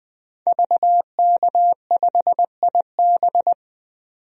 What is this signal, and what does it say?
Morse code, heard as a keyed tone, about 21 wpm, 700 Hz. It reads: VK5IB
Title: DXCC-100 Call Signs - 5 Characters - 20wpm